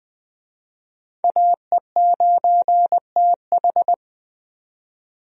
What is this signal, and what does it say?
Morse code, heard as a keyed tone, about 20 wpm, 700 Hz. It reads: AE9TH
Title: USA Call Signs - 5 Characters - 20wpm